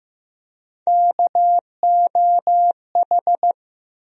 Morse code, heard as a keyed tone, about 15 wpm, 700 Hz. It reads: KOH
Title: USA Call Sign Postfixes - 3 Letters 15wpm